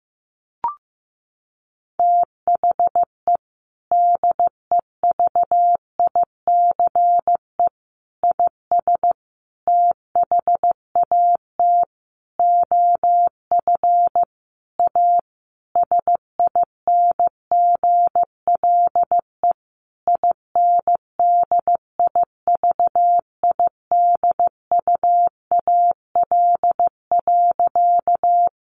Morse code, heard as a keyed tone, about 15 wpm, 700 Hz. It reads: THE DEVICE IS THAT OF A SINGLE INDIVIDUAL.
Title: Sentences from Top 1500 Words 15wpm